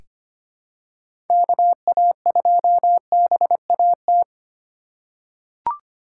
Morse code, heard as a keyed tone, about 25 wpm, 700 Hz. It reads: KA2BAT
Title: USA Call Signs - 6 Characters - 3x Character Spacing - 25wpm